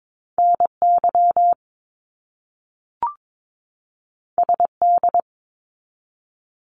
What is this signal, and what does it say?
Morse code, heard as a keyed tone, about 22 wpm, 700 Hz. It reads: NY SD
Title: US States 22wpm